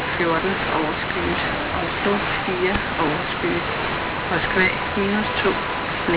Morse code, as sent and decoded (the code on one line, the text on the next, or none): none